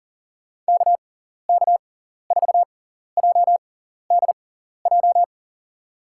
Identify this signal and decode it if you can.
Morse code, heard as a keyed tone, about 40 wpm, spaced as for 14 wpm, 700 Hz. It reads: KK4JDJ
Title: USA Call Signs - 6 Characters - 6x Character Spacing - 40wpm